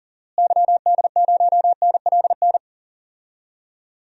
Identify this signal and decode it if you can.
Morse code, heard as a keyed tone, about 40 wpm, 700 Hz. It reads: YD0NLN
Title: DXCC-100 Call Signs - 6 Characters - Repeated 3x - 40wpm